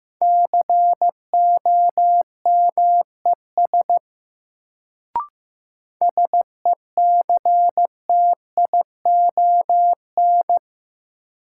Morse code, heard as a keyed tone, about 15 wpm, 700 Hz. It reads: COMES SECTION
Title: Sentences from Top 1000 Words - Review Part 5 - New Word Drill - 15wpm